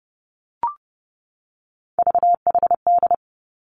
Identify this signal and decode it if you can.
Morse code, heard as a keyed tone, about 32 wpm, 700 Hz. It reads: VHD
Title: Three Challenging Letter-Number Combinations 30wpm